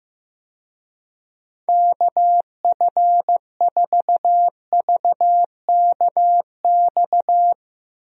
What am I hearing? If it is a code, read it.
Morse code, heard as a keyed tone, about 15 wpm, 700 Hz. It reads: KF4VKX
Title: USA Call Signs - 6 Characters - Repeated 2x - 15wpm